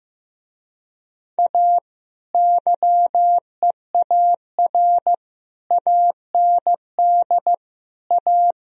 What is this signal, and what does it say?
Morse code, heard as a keyed tone, about 16 wpm, 700 Hz. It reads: A YEAR AND A
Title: Speed-Racing - Sets of 5 Words 15wpm